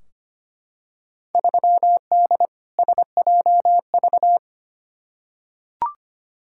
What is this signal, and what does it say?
Morse code, heard as a keyed tone, about 25 wpm, 700 Hz. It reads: 3D SJV
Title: Field Day Exchanges - 25wpm